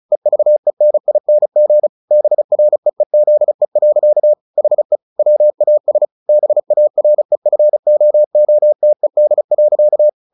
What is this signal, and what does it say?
Morse code, heard as a keyed tone, about 34 wpm, 600 Hz. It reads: EVENING BREEZE. HE WAS BAREFOOTED.
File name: the old man and the sea / the old man and the sea 1 35wpm